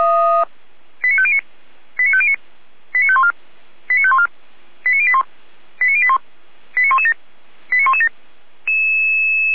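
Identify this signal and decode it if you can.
Five-tone selective call (ZVEI), heard as quick runs of five tones, tones of about 70 ms. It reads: 89509 89509 89514 89514 89081 89081 89117 89117
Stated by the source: ZVEI five-tone call 89105 received by SDR.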